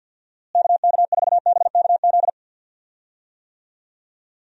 Morse code, heard as a keyed tone, about 51 wpm, 700 Hz. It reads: KK4BKZ
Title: USA Call Signs - 6 Characters - Repeated 3x - 50wpm